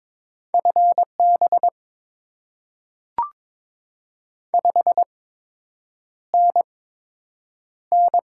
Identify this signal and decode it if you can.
Morse code, heard as a keyed tone, about 22 wpm, spaced as for 8 wpm, 700 Hz. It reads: FB 5NN
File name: Top 20 QSO Elements - 8x Character Spacing - 22wpm